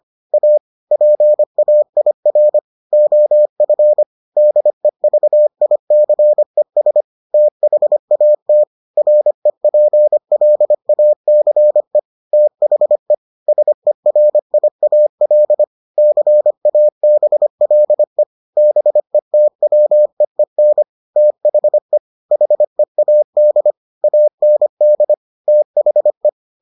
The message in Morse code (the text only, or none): A PAIR OF DEVICES THAT REPLACE THE SERIAL CABLE BETWEEN THE HEAD AND THE